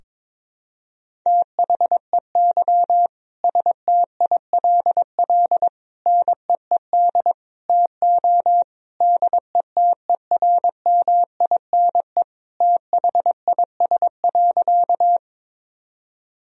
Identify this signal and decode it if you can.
Morse code, heard as a keyed tone, about 22 wpm, 700 Hz. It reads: THEY STILL NEED TO DETERMINE THIS.
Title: Sentences from Top 1000 Words - Review Part 4 - 22wpm